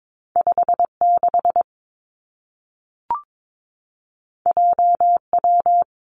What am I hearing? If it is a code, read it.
Morse code, heard as a keyed tone, about 22 wpm, 700 Hz. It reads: 5 6 J W
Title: Two Challenging Letter-Number Combinations 22wpm